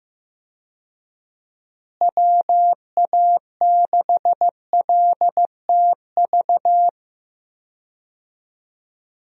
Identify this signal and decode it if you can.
Morse code, heard as a keyed tone, about 14 wpm, 700 Hz. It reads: WA6LTV
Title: USA Call Signs - 6 Characters - Repeated 3x - 15wpm